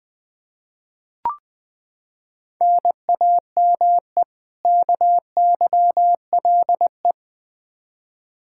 Morse code, heard as a keyed tone, about 20 wpm, 700 Hz. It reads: NAME KYLE
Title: Practice QSO - Exchange 1 - Beginning - 20wpm